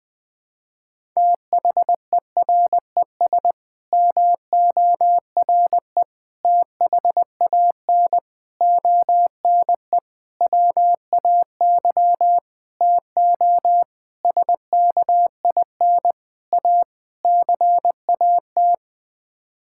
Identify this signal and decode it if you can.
Morse code, heard as a keyed tone, about 20 wpm, 700 Hz. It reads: THERES MORE THAN ONE WAY TO SKIN A CAT
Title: Sets of 9 Words - 20wpm